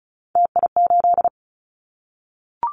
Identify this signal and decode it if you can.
Morse code, heard as a keyed tone, about 34 wpm, 700 Hz. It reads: TI8
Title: Callsign Prefixes for Top 100 DXCC - 35wpm